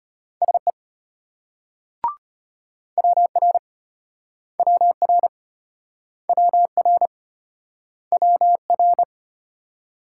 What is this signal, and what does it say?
Morse code, heard as a keyed tone, about 31 wpm, 700 Hz. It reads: SE WR WR WR WR